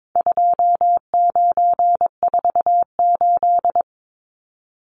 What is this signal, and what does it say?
Morse code, heard as a keyed tone, about 22 wpm, 700 Hz. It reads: 2948
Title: Numbers - 4-Digit 22wpm